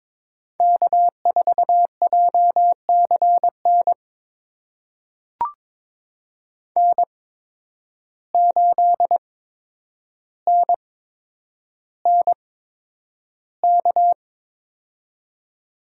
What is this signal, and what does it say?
Morse code, heard as a keyed tone, about 22 wpm, spaced as for 9 wpm, 700 Hz. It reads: K4JCN N8NNK
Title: USA Call Signs - 8x Character Spacing - 22wpm